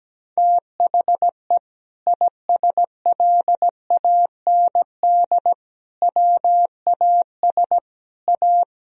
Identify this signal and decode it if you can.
Morse code, heard as a keyed tone, about 17 wpm, 700 Hz. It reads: THE ISLAND WAS A
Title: Sentences from Top 500 Words and Abbreviations 17wpm